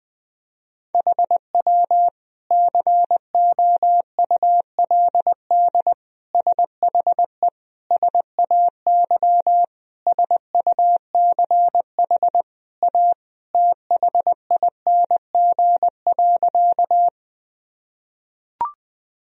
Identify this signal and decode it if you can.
Morse code, heard as a keyed tone, about 21 wpm, 700 Hz. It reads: HW COULD SHE SAY SUCH A THING.
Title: Sentences from Top 200 Words and Abbreviations 20wpm